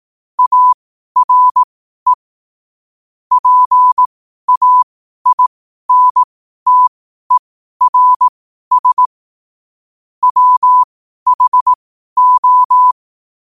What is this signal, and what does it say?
Morse code, heard as a keyed tone, about 18 wpm, spaced as for 12 wpm, 1000 Hz. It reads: ARE PAINTERS WHO
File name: qotdfiles / 12wpm